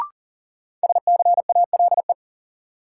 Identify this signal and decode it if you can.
Morse code, heard as a keyed tone, about 40 wpm, 700 Hz. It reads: SCALE
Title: Sentences from Top 1000 Words - Review Part 4 - New Word Drill - 40wpm